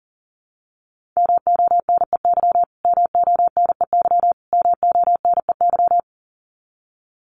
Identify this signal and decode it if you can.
Morse code, heard as a keyed tone, about 40 wpm, 700 Hz. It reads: MONEY MONEY MONEY
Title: Top 500 Words Repeated 40wpm